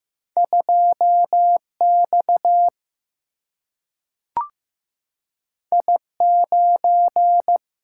Morse code, heard as a keyed tone, about 15 wpm, 700 Hz. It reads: 2X I9
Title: Single-Letter Single-Number 15wpm